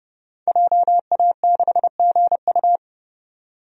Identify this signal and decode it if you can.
Morse code, heard as a keyed tone, about 32 wpm, 700 Hz. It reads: JA6GU
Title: DXCC-100 Call Signs - 5 Characters - 30wpm